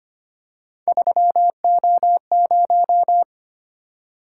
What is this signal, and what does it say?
Morse code, heard as a keyed tone, about 25 wpm, 700 Hz. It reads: 3O0
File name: Three Letter-Number Combinations - 25wpm